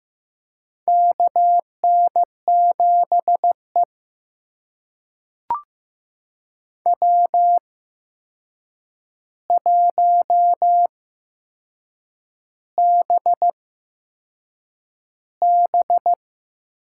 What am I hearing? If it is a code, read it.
Morse code, heard as a keyed tone, about 15 wpm, spaced as for 6 wpm, 700 Hz. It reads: KN7E W1BB
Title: USA Call Signs - 4 Characters - 8x Character Spacing - 15wpm